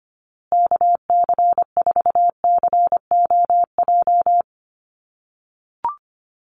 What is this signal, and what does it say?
Morse code, heard as a keyed tone, about 25 wpm, 700 Hz. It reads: KC4COJ
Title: USA Call Signs Repeated 2x 25wpm